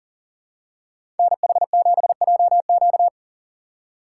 Morse code, heard as a keyed tone, about 40 wpm, 700 Hz. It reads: NH7JQ
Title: USA Call Signs - 5 Characters - 3x Character Spacing - 40wpm